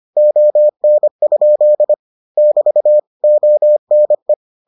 Morse code, heard as a keyed tone, about 25 wpm, 600 Hz. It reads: ON? <BT> ONE